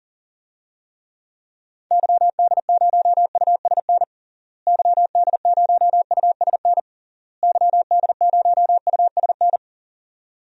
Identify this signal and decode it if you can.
Morse code, heard as a keyed tone, about 37 wpm, 700 Hz. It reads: YD0USN YD0USN YD0USN
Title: DXCC-100 Call Signs - 6 Characters - Repeated 3x - 40wpm